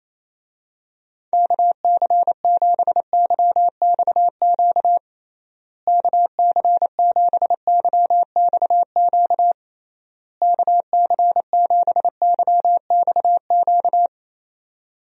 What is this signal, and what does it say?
Morse code, heard as a keyed tone, about 28 wpm, 700 Hz. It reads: KC7YXQ KC7YXQ KC7YXQ
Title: USA Call Signs - 6 Characters - Repeated 3x - 28wpm